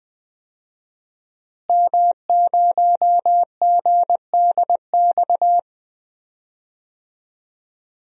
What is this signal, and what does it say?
Morse code, heard as a keyed tone, about 20 wpm, 700 Hz. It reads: M0GDX
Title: DXCC-100 Call Signs - 5 Characters - Repeated 2x - 20wpm